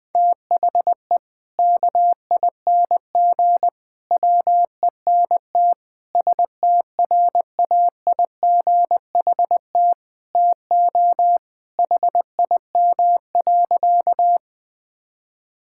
Morse code, heard as a keyed tone, about 21 wpm, 700 Hz. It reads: THE KING WENT STRAIGHT TO HIM.
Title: Sentences from Top 1000 Words - Review Part 6 - 20wpm